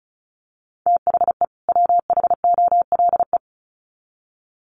Morse code, heard as a keyed tone, about 34 wpm, 700 Hz. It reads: THE WHOLE
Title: Sets of 2 Words 35wpm